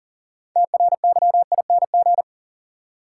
Morse code, heard as a keyed tone, about 40 wpm, 700 Hz. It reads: TRYING
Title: Sets of 2 Words - New Word Drill - 40wpm